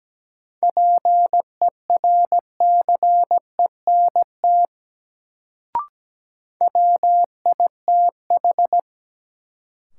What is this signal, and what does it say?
Morse code, heard as a keyed tone, about 17 wpm, 700 Hz. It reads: PERCENT WITH
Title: Sets of 2 Words - New Word Drill - 17wpm